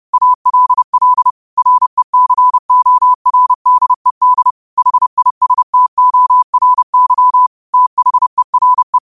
Morse code, heard as a keyed tone, about 30 wpm, 1000 Hz. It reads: ALL RECORDED HISTORY THERE
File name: qotdfiles / 30wpm